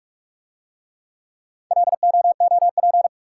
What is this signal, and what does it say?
Morse code, heard as a keyed tone, about 44 wpm, 700 Hz. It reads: LOOP